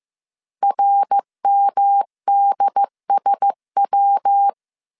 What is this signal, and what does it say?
Morse code, heard as a keyed tone, about 15 wpm, 800 Hz. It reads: RMDSW
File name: CW / DET / 1 6 1500C S C 14wpm 2951